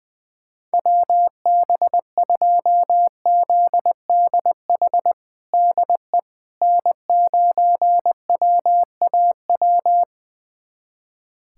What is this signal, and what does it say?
Morse code, heard as a keyed tone, about 21 wpm, 700 Hz. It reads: WB2ZDH DE N9WAW
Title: USA Callsign DE Callsign 20wpm